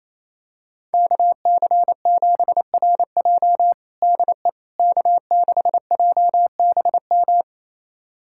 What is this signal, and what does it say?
Morse code, heard as a keyed tone, about 28 wpm, 700 Hz. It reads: KC7RJ DE K6JBM
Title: USA Callsign DE Callsign - 5 Characters - 28wpm